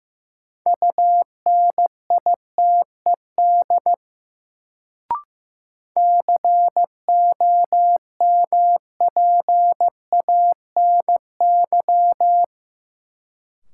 Morse code, heard as a keyed tone, about 15 wpm, 700 Hz. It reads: UNITED COMPANY
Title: Sets of 2 Words - New Word Drill - 15wpm